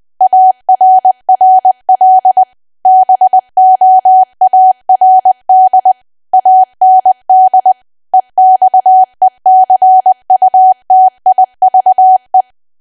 Morse code, heard as a keyed tone, about 20 wpm, 750 Hz. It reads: ARRL BOARD AND EXECUTIVE
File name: cwa / W1AW-Level3 / 20WPM